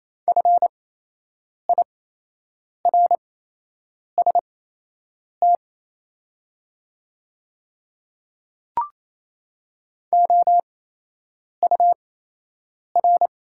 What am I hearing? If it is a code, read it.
Morse code, heard as a keyed tone, about 27 wpm, spaced as for 8 wpm, 700 Hz. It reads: FIRST OUR
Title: 8x Character Spacing - Top 200 Words - 28wpm